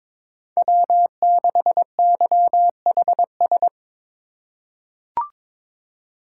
Morse code, heard as a keyed tone, about 22 wpm, 700 Hz. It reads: W6YHS
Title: USA Call Signs - 5 Characters - 4x Character Spacing - 22wpm